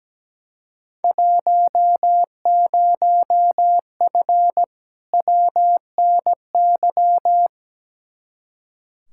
Morse code, heard as a keyed tone, about 17 wpm, 700 Hz. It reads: 10F WNY
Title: Field Day Exchanges Encore - 17wpm